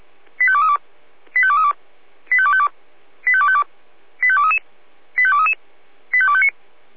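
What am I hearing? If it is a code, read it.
Five-tone selective call (ZVEI), heard as quick runs of five tones, tones of about 70 ms. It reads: 85232 85232 85252 85252 85233 85233 85259
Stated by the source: ZVEI five-tone selective call received by SDR, feuerwehr-orlamuende.org fire brigade alerting archive.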